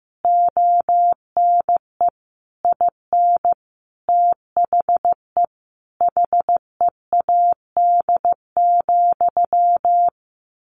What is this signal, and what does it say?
Morse code, heard as a keyed tone, about 15 wpm, 700 Hz. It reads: ONE IN THE HEAD,
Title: A Sound Of Thunder 15wpm